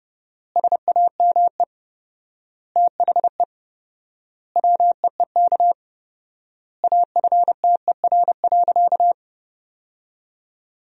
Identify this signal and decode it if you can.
Morse code, heard as a keyed tone, about 30 wpm, 700 Hz. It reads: SAME THE WEEK AFTER.
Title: 4x Word-Spacing - Sentences from Top 500 Words - 30wpm